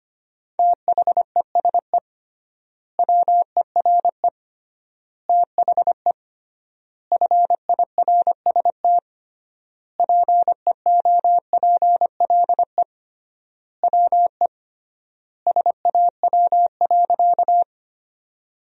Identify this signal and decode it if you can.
Morse code, heard as a keyed tone, about 25 wpm, 700 Hz. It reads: THESE WERE THE FIRST PEOPLE WE SAW.
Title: 3x Word-Spacing - Sentences from Top 300 Words - 25wpm